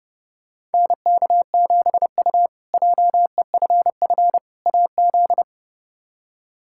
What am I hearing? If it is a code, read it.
Morse code, heard as a keyed tone, about 30 wpm, 700 Hz. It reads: NK7U JEFF AZ
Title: K1USN SST QSO Practice - 30wpm